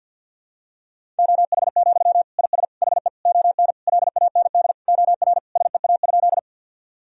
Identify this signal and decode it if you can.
Morse code, heard as a keyed tone, about 50 wpm, 700 Hz. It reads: OH, IS HE ON LAND OR SEA?
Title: Sentences from Top 1000 Words - Review Part 1 - 50wpm